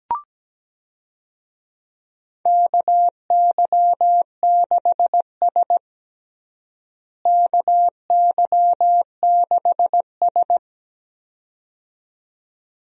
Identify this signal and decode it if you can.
Morse code, heard as a keyed tone, about 17 wpm, 700 Hz. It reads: KY6S KY6S
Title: USA Call Signs - 4 Characters - Repeated 2x - 17wpm